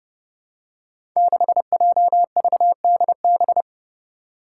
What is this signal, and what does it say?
Morse code, heard as a keyed tone, about 29 wpm, 700 Hz. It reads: 6JVDB
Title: Five Challenging Letter-Number Combinations 30wpm